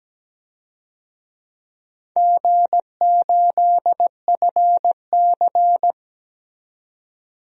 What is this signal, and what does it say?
Morse code, heard as a keyed tone, about 17 wpm, 700 Hz. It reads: G8FC
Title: DXCC-100 Call Signs - 4 Characters - 17wpm